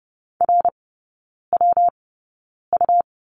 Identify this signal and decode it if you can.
Morse code, heard as a keyed tone, about 30 wpm, spaced as for 9 wpm, 700 Hz. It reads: RWU
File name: Lesson 19e - Letter w - Three Characters 30wpm